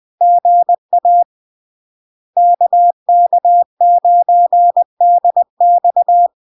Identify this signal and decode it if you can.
Morse code, heard as a keyed tone, about 20 wpm, 700 Hz. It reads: GA KK9DX
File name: LARK-FD-2022-Log-in-CW-20wpm-700-Hz